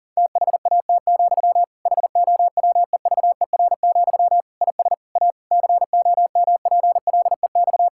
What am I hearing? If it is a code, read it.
Morse code, heard as a keyed tone, about 40 wpm, 700 Hz. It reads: THAT, HOWEVER, IS A COMPLEX
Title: Sentences from Top 1000 Words - Review Part 6 - 40wpm